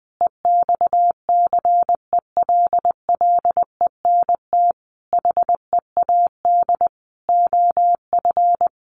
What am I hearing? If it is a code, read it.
Morse code, heard as a keyed tone, about 20 wpm, 700 Hz. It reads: EXCELLENT HEAD OF
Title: Sentences from Top 1200 Words 20wpm